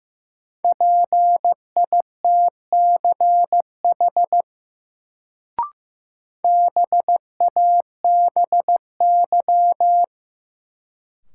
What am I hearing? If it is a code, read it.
Morse code, heard as a keyed tone, about 15 wpm, 700 Hz. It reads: PITCH BABY
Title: Sentences from Top 1300 Words - New Word Drill - 15wpm